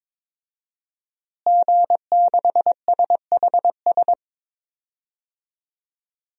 Morse code, heard as a keyed tone, about 22 wpm, 700 Hz. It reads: G6SHS
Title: DXCC-100 Call Signs - 5 Characters - Repeated 3x - 22wpm